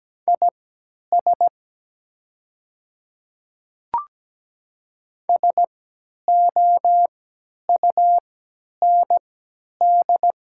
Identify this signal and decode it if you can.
Morse code, heard as a keyed tone, about 17 wpm, spaced as for 10 wpm, 700 Hz. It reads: IS SOUND
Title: 3x Character Spacing - Top 100 Words - 17wpm